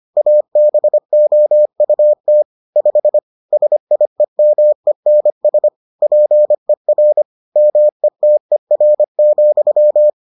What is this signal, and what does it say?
Morse code, heard as a keyed tone, about 25 wpm, 600 Hz. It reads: ABOUT 5 SIEMENS PER METER,